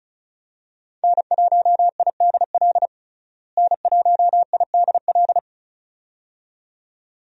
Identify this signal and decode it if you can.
Morse code, heard as a keyed tone, about 34 wpm, 700 Hz. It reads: N1IDL N1IDL
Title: USA Call Signs - 5 Characters - Repeated 2x - 35wpm